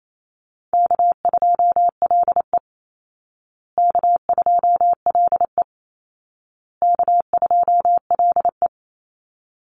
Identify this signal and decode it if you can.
Morse code, heard as a keyed tone, about 28 wpm, 700 Hz. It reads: K2LE K2LE K2LE
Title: Easy Call Signs Repeated 3x - 28wpm